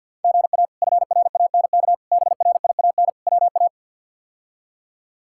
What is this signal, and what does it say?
Morse code, heard as a keyed tone, about 52 wpm, 700 Hz. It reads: GA FRANK BRIAN WA
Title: K1USN SST QSO Practice - 50wpm